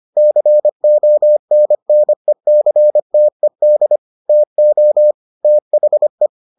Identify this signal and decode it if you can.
Morse code, heard as a keyed tone, about 25 wpm, 600 Hz. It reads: CONNECTED TO THE